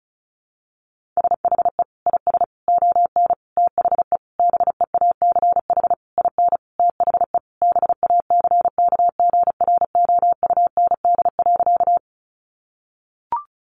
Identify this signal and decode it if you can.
Morse code, heard as a keyed tone, about 34 wpm, 700 Hz. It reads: SHE IS ON THE BEACH IN THE BACKGROUND.
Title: Sentences from Top 1500 Words 35wpm